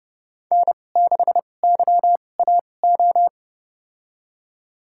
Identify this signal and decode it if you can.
Morse code, heard as a keyed tone, about 30 wpm, spaced as for 22 wpm, 700 Hz. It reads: N6YAO
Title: USA Call Signs - 5 Characters - 2x Character Spacing - 30wpm